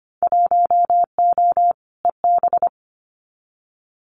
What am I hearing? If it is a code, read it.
Morse code, heard as a keyed tone, about 25 wpm, 700 Hz. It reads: 1O EB
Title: Winter Field Day Exchanges 25wpm